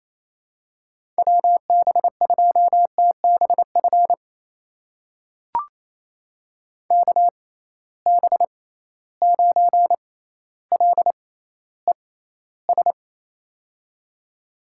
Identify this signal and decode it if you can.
Morse code, heard as a keyed tone, about 28 wpm, spaced as for 14 wpm, 700 Hz. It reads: WB2TBF KB9LES
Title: USA Call Signs - 6 Characters - 6x Character Spacing - 28wpm